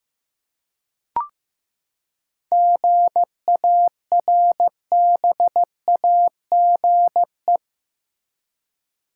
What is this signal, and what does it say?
Morse code, heard as a keyed tone, about 15 wpm, 700 Hz. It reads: GARBAGE
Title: Seven Letter Words - 15wpm